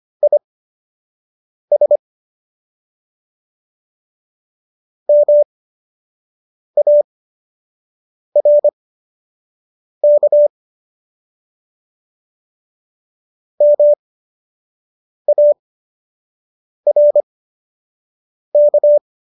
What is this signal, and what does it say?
Morse code, heard as a keyed tone, about 25 wpm, spaced as for 6 wpm, 590 Hz. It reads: IS MARK MARK